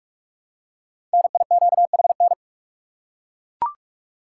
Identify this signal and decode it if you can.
Morse code, heard as a keyed tone, about 41 wpm, 700 Hz. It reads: NIQHN
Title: Code Groups - 5 Letter:Numbers - 45wpm